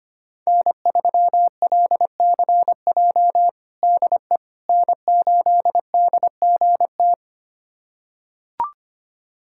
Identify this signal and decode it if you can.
Morse code, heard as a keyed tone, about 25 wpm, 700 Hz. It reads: N3LCJ DE N8DGT
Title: USA Callsign DE Callsign 25wpm